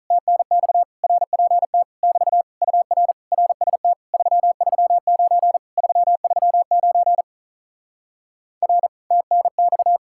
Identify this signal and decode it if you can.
Morse code, heard as a keyed tone, about 48 wpm, 700 Hz. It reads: TNX RPT <BT> UR RST 339 339 R TNX